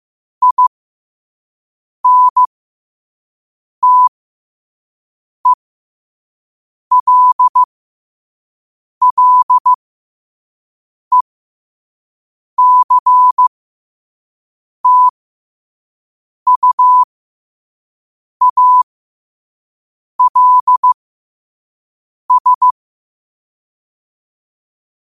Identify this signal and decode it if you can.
Morse code, heard as a keyed tone, about 15 wpm, spaced as for 5 wpm, 1000 Hz. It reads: INTELLECTUALS
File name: qotdfiles / 5wpm